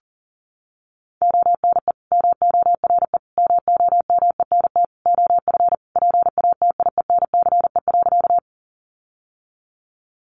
Morse code, heard as a keyed tone, about 40 wpm, 700 Hz. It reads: ONE MORE MOMENT OF PATIENCE.
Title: Sentences from Top 1300 Words 40wpm